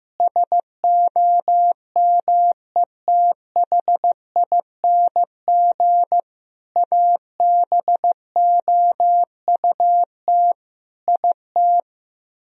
Morse code, heard as a keyed tone, about 15 wpm, 700 Hz. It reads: SOMETHING ABOUT IT
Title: Speed-Racing - Sets of 5 Words 15wpm